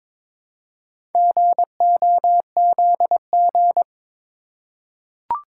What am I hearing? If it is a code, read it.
Morse code, heard as a keyed tone, about 22 wpm, 700 Hz. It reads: GOZG
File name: practice / 22wpm / Four Letter-Number Combinations - 22wpm